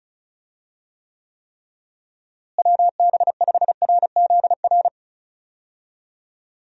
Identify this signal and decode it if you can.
Morse code, heard as a keyed tone, about 34 wpm, 700 Hz. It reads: WB5RZR
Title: USA Call Signs - 6 Characters - Repeated 3x - 35wpm